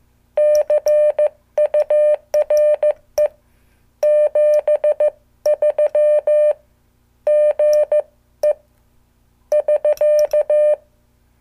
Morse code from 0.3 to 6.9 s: CURE 73